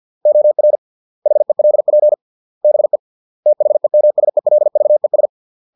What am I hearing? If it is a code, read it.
Morse code, heard as a keyed tone, about 50 wpm, 600 Hz. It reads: OR HELP BE THEMSELVES